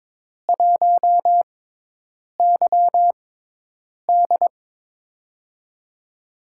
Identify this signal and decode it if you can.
Morse code, heard as a keyed tone, about 23 wpm, spaced as for 8 wpm, 700 Hz. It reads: 1YD
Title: USA Call Signs - 5 Characters - 6x Character Spacing - 22wpm